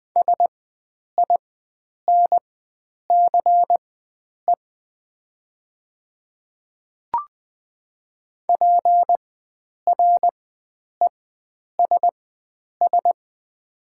Morse code, heard as a keyed tone, about 21 wpm, spaced as for 9 wpm, 700 Hz. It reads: SINCE PRESS
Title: 4x Character Spacing - Top 300 Words - 20wpm